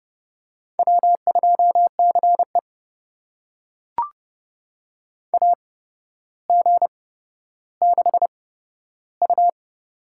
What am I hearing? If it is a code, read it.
Morse code, heard as a keyed tone, about 30 wpm, spaced as for 13 wpm, 700 Hz. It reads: W2CE AG6U
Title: USA Call Signs - 4 Characters - 8x Character Spacing - 30wpm